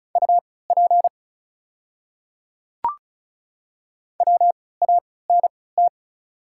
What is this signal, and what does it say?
Morse code, heard as a keyed tone, about 36 wpm, spaced as for 20 wpm, 700 Hz. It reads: UP WANT
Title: 3x Character Spacing - Top 200 Words - 35wpm